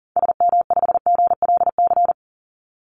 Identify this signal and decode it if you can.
Morse code, heard as a keyed tone, about 40 wpm, 700 Hz. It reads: SM5GLC
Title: Call Signs - 40wpm